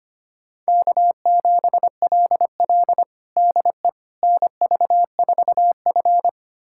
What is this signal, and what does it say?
Morse code, heard as a keyed tone, about 25 wpm, 700 Hz. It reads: K7LL DE NV4F
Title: USA Callsign DE Callsign - 4 Characters - 25wpm